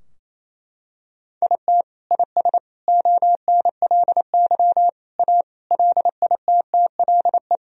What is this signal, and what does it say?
Morse code, heard as a keyed tone, about 28 wpm, 700 Hz. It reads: IT IS ONLY A LITTLE
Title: Sentences from Top 1000 Words - Review Part 2 - 28wpm